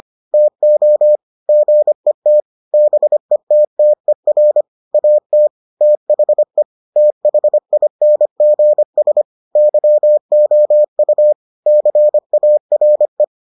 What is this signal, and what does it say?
Morse code, heard as a keyed tone, about 25 wpm, 600 Hz. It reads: TO GET BETTER AT THE THINGS YOU CARE